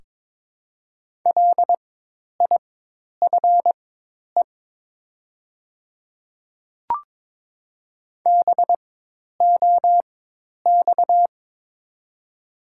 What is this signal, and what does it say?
Morse code, heard as a keyed tone, about 22 wpm, spaced as for 10 wpm, 700 Hz. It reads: LIFE BOX
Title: 4x Character Spacing - Top 500 Words - 22wpm